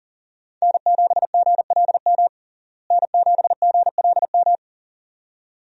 Morse code, heard as a keyed tone, about 40 wpm, 700 Hz. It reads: N7GLM N7GLM
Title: USA Call Signs - 5 Characters - Repeated 2x - 40wpm